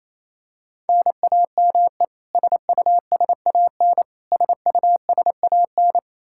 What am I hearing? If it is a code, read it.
Morse code, heard as a keyed tone, about 28 wpm, 700 Hz. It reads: NAME SUSAN SUSAN
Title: Practice QSO - Exchange 1 - Beginning - 28wpm